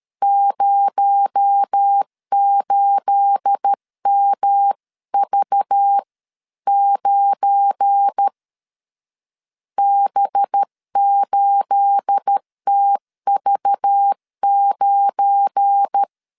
Morse code, heard as a keyed tone, about 13 wpm, 800 Hz. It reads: 08MV9 B8TV9
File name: LEZIONI / 9 28 12wpm 2351